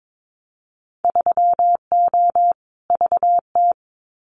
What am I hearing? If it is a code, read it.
Morse code, heard as a keyed tone, about 22 wpm, 700 Hz. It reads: 3O VT
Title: Winter Field Day Exchanges Encore 22wpm